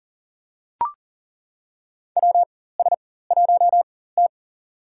Morse code, heard as a keyed tone, about 40 wpm, spaced as for 19 wpm, 700 Hz. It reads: WS1T
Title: USA Call Signs - 4 Characters - 4x Character Spacing - 40wpm